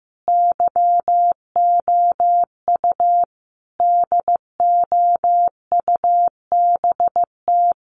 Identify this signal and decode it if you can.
Morse code, heard as a keyed tone, about 15 wpm, 700 Hz. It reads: YOU DOUBT
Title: Sentences from Top 1500 Words 15wpm